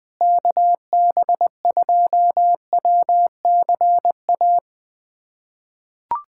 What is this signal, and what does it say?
Morse code, heard as a keyed tone, about 21 wpm, 700 Hz. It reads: KB2WCA
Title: USA Callsign DE Callsign 20wpm